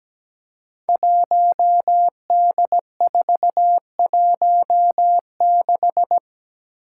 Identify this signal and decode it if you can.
Morse code, heard as a keyed tone, about 17 wpm, 700 Hz. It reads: 1D416
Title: Five Challenging Letter-Number Combinations 17wpm